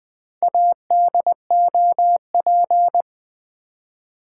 Morse code, heard as a keyed tone, about 20 wpm, 700 Hz. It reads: ADOP
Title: Build A Word - 1 Letter At A Time - 3 to 5-Letter Words - 20wpm